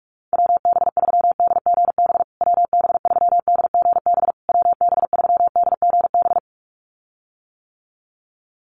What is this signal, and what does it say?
Morse code, heard as a keyed tone, about 44 wpm, 700 Hz. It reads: WB3DGB WB3DGB WB3DGB
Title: USA Call Signs Repeated 3x 45wpm